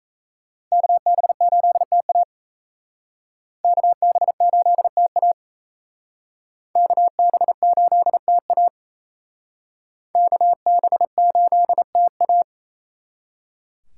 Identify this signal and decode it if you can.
Morse code, heard as a keyed tone, about 37 wpm, 700 Hz. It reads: KB8TA KB8TA KB8TA KB8TA